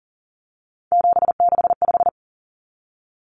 Morse code, heard as a keyed tone, about 40 wpm, 700 Hz. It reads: 765
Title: Numbers - 3-Digit 40wpm